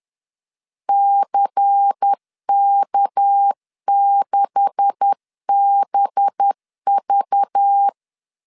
Morse code, heard as a keyed tone, about 10 wpm, 800 Hz. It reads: CK6BV
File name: LEZIONI / 9 286 10wpm 2735